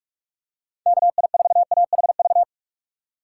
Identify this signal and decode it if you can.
Morse code, heard as a keyed tone, about 46 wpm, 700 Hz. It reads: K I 4 A H V
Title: USA Call Signs - 6 Characters - 6x Character Spacing - 45wpm